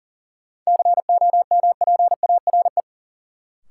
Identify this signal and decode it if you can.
Morse code, heard as a keyed tone, about 40 wpm, 700 Hz. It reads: COMPARE
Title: Sentences from Top 1300 Words - New Word Drill - 40wpm